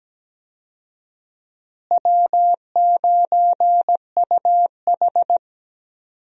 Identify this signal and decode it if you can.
Morse code, heard as a keyed tone, about 17 wpm, 700 Hz. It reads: W9UH
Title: USA Call Signs - 4 Characters - Repeated 2x - 17wpm